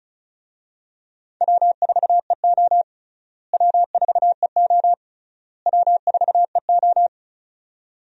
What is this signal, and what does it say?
Morse code, heard as a keyed tone, about 36 wpm, 700 Hz. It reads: W4EO W4EO W4EO
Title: USA Call Signs - 4 Characters - Repeated 3x - 35wpm